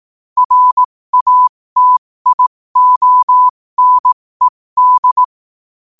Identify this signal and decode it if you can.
Morse code, heard as a keyed tone, about 18 wpm, spaced as for 15 wpm, 1000 Hz. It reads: RATIONED